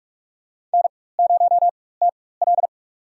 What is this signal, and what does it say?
Morse code, heard as a keyed tone, about 44 wpm, spaced as for 21 wpm, 700 Hz. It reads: N0TL
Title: USA Call Signs - 4x Character Spacing - 45wpm